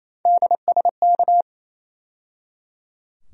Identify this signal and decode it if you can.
Morse code, heard as a keyed tone, about 28 wpm, 700 Hz. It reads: DSK
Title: USA Call Sign Postfixes - 3 Letters 28wpm